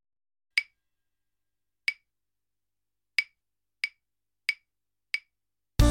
There is no code.